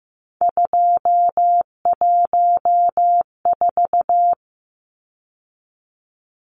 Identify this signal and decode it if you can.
Morse code, heard as a keyed tone, about 15 wpm, 700 Hz. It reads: 214
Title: Numbers - 3-Digit 15wpm